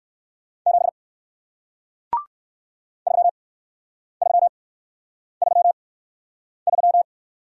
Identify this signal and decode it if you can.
Morse code, heard as a keyed tone, about 58 wpm, spaced as for 9 wpm, 700 Hz. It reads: 7 3333